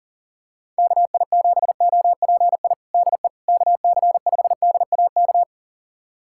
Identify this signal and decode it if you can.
Morse code, heard as a keyed tone, about 37 wpm, 700 Hz. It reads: KI7OPI DE KC5DAK
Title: USA Callsign DE Callsign 40wpm